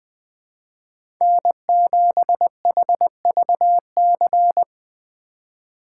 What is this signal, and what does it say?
Morse code, heard as a keyed tone, about 20 wpm, 700 Hz. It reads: N7HVC
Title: USA Call Signs - 2x Character Spacing - 20wpm